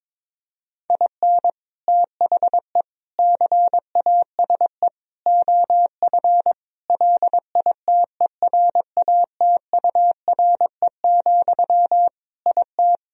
Morse code, heard as a keyed tone, about 22 wpm, 700 Hz. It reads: IN THE CASE OF LITERATURE, IT
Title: Sentences from Top 1000 Words - Review Part 3 - 22wpm